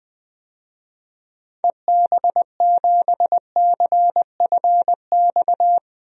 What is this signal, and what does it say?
Morse code, heard as a keyed tone, about 20 wpm, 700 Hz. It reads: EB7CFX
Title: DXCC-100 Call Signs - 6 Characters - 20wpm